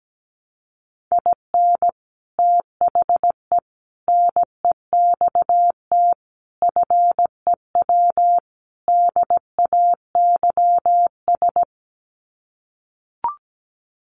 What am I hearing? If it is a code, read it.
Morse code, heard as a keyed tone, about 17 wpm, 700 Hz. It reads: IN THE NEXT FEW DAYS
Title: Sets of 5 Words Encore2 17wpm